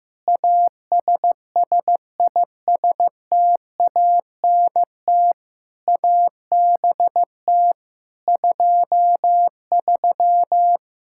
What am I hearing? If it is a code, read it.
Morse code, heard as a keyed tone, about 15 wpm, 700 Hz. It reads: ASSISTANT ABT 23
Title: Practice QSO - Exchange 3 - More Info - 15wpm